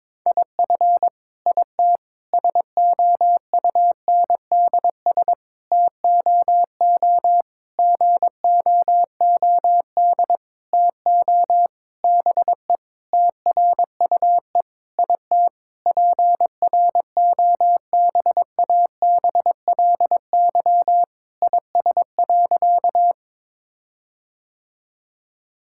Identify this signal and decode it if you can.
Morse code, heard as a keyed tone, about 22 wpm, 700 Hz. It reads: IF IT SOUNDS TOO GOOD TO BE TRUE IT PROBABLY IS.
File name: Sentences with 9-Word Phrases - 22wpm